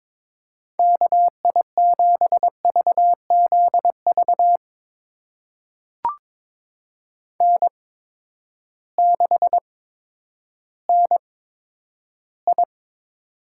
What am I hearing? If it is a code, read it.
Morse code, heard as a keyed tone, about 22 wpm, spaced as for 11 wpm, 700 Hz. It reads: KI7VZV N6NI
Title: USA Call Signs - 8x Character Spacing - 22wpm